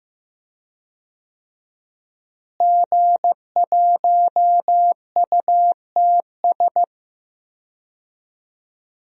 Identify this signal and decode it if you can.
Morse code, heard as a keyed tone, about 16 wpm, 700 Hz. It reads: G1UTS
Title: DXCC-100 Call Signs - 5 Characters - Repeated 2x - 15wpm